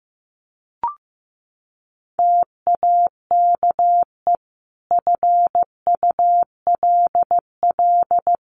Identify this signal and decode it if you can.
Morse code, heard as a keyed tone, about 15 wpm, 700 Hz. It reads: TAKE FULL
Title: Sentences from Top 1400 Words 15wpm